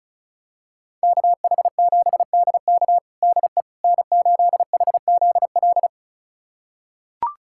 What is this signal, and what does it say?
Morse code, heard as a keyed tone, about 36 wpm, 700 Hz. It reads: KH7DK DE N8HZL
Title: USA Callsign DE Callsign - 5 Characters - 35wpm